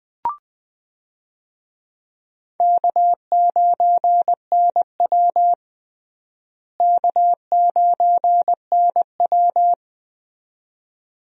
Morse code, heard as a keyed tone, about 20 wpm, 700 Hz. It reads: K9NW K9NW
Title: USA Call Signs - 4 Characters - Repeated 2x - 20wpm